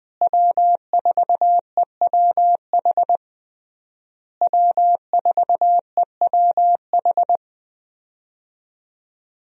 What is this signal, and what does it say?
Morse code, heard as a keyed tone, about 20 wpm, 700 Hz. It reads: W4EWH W4EWH
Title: USA Call Signs - 5 Characters - Repeated 3x - 20wpm